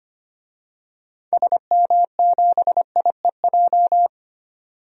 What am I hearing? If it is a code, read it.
Morse code, heard as a keyed tone, about 25 wpm, 690 Hz. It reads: SM7IEJ
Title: DXCC-100 Call Signs - 6 Characters - 25wpm